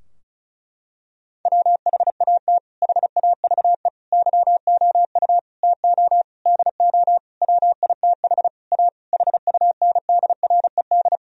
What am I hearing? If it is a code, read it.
Morse code, heard as a keyed tone, about 36 wpm, 700 Hz. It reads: WHAT HAVE YOU TO DO WITH A HUNDRED
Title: Sentences from Top 1000 Words - Review Part 2 - 35wpm